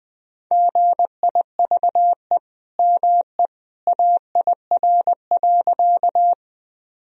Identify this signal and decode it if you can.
Morse code, heard as a keyed tone, about 20 wpm, 700 Hz. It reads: GIVE ME AIR.
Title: Sentences from Top 1000 Words - Review Part 1 - 20wpm